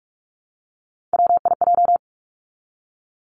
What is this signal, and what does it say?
Morse code, heard as a keyed tone, about 46 wpm, 700 Hz. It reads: WIJ
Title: Three Challenging Letter-Number Combinations 45wpm